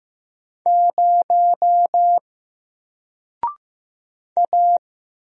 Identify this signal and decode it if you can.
Morse code, heard as a keyed tone, about 15 wpm, 700 Hz. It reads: 0 A
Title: Single Letter-Number Alt 15wpm